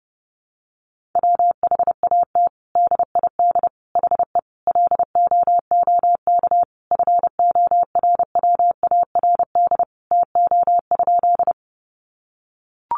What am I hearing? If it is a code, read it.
Morse code, heard as a keyed tone, about 30 wpm, 700 Hz. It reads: WHAT DID HE LOOK FORWARD TO?
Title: Sentences from Top 1300 Words 30wpm